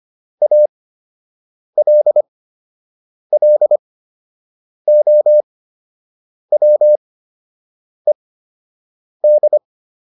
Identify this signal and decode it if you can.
Morse code, heard as a keyed tone, about 25 wpm, spaced as for 7 wpm, 600 Hz. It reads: ALLOWED